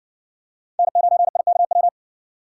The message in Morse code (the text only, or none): N9IKW